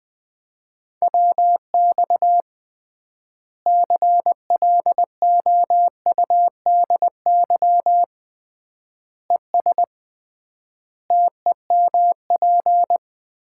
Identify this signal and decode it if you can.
Morse code, heard as a keyed tone, about 20 wpm, 700 Hz. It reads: WX CLOUDY ES TEMP
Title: Common QSO Phrases - 3x Word-Spacing - 20wpm